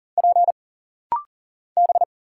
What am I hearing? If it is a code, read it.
Morse code, heard as a keyed tone, about 40 wpm, 700 Hz. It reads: P B
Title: Single Letters - Rapid-Fire 40wpm